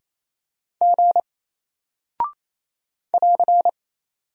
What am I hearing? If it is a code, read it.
Morse code, heard as a keyed tone, about 28 wpm, 700 Hz. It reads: G <AR>
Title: Single Character (Letter:Number:Punctuation:Prosign) Alt 28wpm